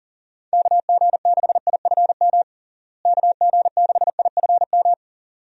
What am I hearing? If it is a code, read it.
Morse code, heard as a keyed tone, about 40 wpm, 700 Hz. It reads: KG6IFM KG6IFM
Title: USA Call Signs - 6 Characters - Repeated 2x - 40wpm